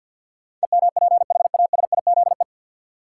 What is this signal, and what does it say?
Morse code, heard as a keyed tone, about 53 wpm, 700 Hz. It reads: EMPHASIZE
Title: Sentences from Top 1500 Words - New Word Drill - 50wpm